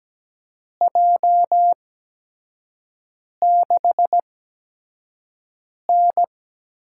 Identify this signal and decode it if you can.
Morse code, heard as a keyed tone, about 17 wpm, spaced as for 5 wpm, 700 Hz. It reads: J6N
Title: USA Call Signs - 8x Character Spacing - 17wpm